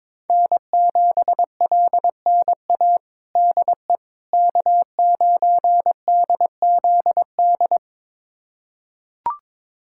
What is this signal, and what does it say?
Morse code, heard as a keyed tone, about 22 wpm, 700 Hz. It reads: N7LNA DE K9DZD
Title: USA Callsign DE Callsign - 5 Characters - 22wpm